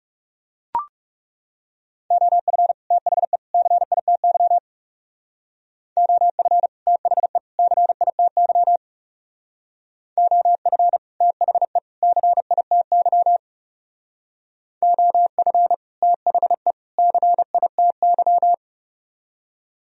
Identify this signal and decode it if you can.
Morse code, heard as a keyed tone, about 39 wpm, 700 Hz. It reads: OF THE CITY OF THE CITY OF THE CITY OF THE CITY